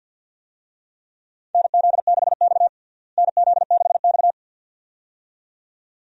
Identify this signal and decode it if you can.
Morse code, heard as a keyed tone, about 52 wpm, 700 Hz. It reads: NZ6X NZ6X
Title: USA Call Signs - 4 Characters - Repeated 2x - 50wpm